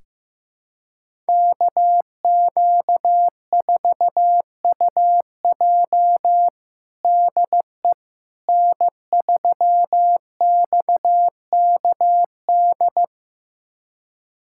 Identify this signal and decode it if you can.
Morse code, heard as a keyed tone, about 15 wpm, 700 Hz. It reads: KQ4UJ DE N3XKD
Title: USA Callsign DE Callsign - 5 Characters - 15wpm